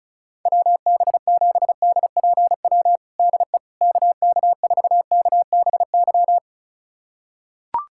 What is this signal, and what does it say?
Morse code, heard as a keyed tone, about 36 wpm, 700 Hz. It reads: WB7DPW DE KK4KBY
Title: USA Callsign DE Callsign - 6 Characters - 35wpm